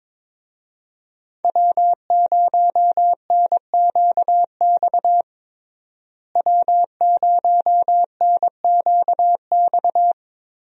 Morse code, heard as a keyed tone, about 22 wpm, 700 Hz. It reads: W0NQX W0NQX
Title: USA Call Signs - 5 Characters - Repeated 2x - 22wpm